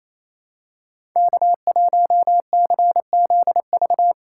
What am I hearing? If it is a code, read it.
Morse code, heard as a keyed tone, about 28 wpm, 700 Hz. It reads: K1CZV
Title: USA Callsign DE Callsign - 5 Characters - 28wpm